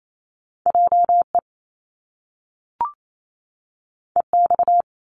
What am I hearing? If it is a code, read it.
Morse code, heard as a keyed tone, about 28 wpm, 700 Hz. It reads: J E E X
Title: Top Two Letter Combinations in Words 28wpm